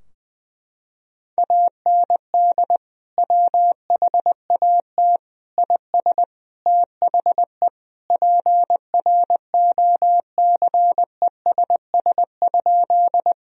Morse code, heard as a keyed tone, about 21 wpm, 700 Hz. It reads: AND WHAT IS THE PROCESS?
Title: Sentences from Top 1000 Words - Review Part 3 - 20wpm